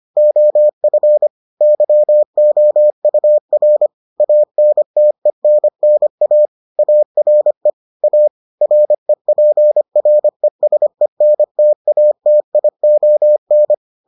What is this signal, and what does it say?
Morse code, heard as a keyed tone, about 25 wpm, 600 Hz. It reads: OF YOUR ANTENNA ARE A REPRESENTATION